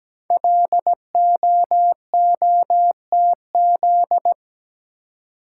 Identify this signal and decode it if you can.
Morse code, heard as a keyed tone, about 17 wpm, 700 Hz. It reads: LOOTZ
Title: Code Groups - 5 Letters - 17wpm